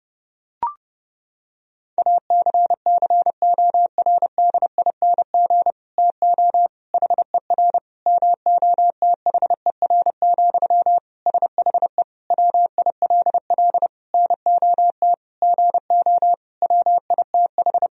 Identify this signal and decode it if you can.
Morse code, heard as a keyed tone, about 30 wpm, 700 Hz. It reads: ACCORDING TO HER MOTHER, SHE WILL NOT GO WITH
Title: Sentences from Top 1000 Words - Review Part 4 - 30wpm